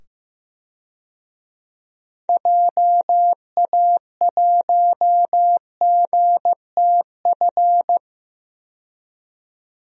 Morse code, heard as a keyed tone, about 15 wpm, 700 Hz. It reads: JA1GTF
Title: DXCC-100 Call Signs - 6 Characters - 15wpm